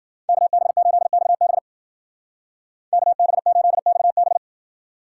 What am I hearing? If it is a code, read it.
Morse code, heard as a keyed tone, about 50 wpm, 700 Hz. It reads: KB8XB KB8XB
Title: Speed Racing - USA Call Signs 40wpm